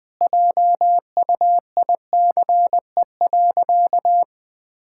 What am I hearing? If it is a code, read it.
Morse code, heard as a keyed tone, about 21 wpm, 700 Hz. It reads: JUICE.
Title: Easy Sentences with Hints - 20wpm